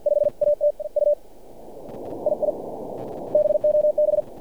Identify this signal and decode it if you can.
Morse code, heard as a keyed tone, about 47 wpm, 600 Hz. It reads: 5 AT U II BOB
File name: archived / CRSnet